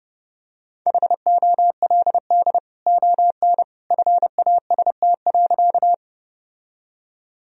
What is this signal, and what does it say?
Morse code, heard as a keyed tone, about 30 wpm, 700 Hz. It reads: HOLD ON FAST.
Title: Sentences from Top 1000 Words - Review Part 2 - 30wpm